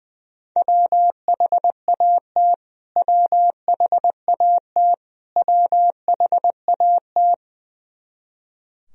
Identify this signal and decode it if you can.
Morse code, heard as a keyed tone, about 20 wpm, 700 Hz. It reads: WHAT WHAT WHAT
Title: Top 100 Words Repeated 20wpm